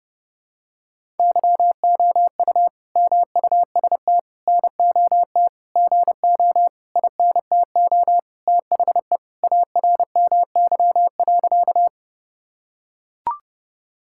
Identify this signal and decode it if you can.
Morse code, heard as a keyed tone, about 28 wpm, 700 Hz. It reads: YOU MUST NOT GO INTO THE ARMY.
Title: Sentences from Top 1000 Words - Review Part 5 - 30wpm